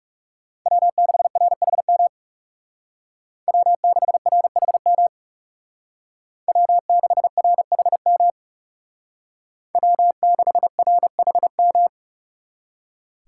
Morse code, heard as a keyed tone, about 38 wpm, 700 Hz. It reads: W6RHM W6RHM W6RHM W6RHM